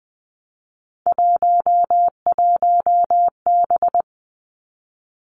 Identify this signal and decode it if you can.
Morse code, heard as a keyed tone, about 20 wpm, 700 Hz. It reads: 11B
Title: Three Challenging Letter-Number Combinations 20wpm